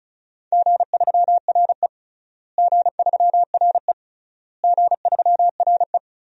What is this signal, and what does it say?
Morse code, heard as a keyed tone, about 36 wpm, 700 Hz. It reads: G3RE G3RE G3RE
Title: DXCC-100 Call Signs - 4 Characters - Repeated 3x - 35wpm